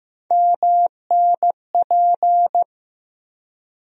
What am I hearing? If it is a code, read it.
Morse code, heard as a keyed tone, about 15 wpm, 700 Hz. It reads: MNP
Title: USA Call Sign Postfixes - 3 Letters 15wpm